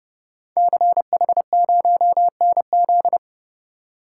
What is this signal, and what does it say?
Morse code, heard as a keyed tone, about 30 wpm, 700 Hz. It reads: CH0NZ
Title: Code Groups - 5 Letter:Numbers - 30wpm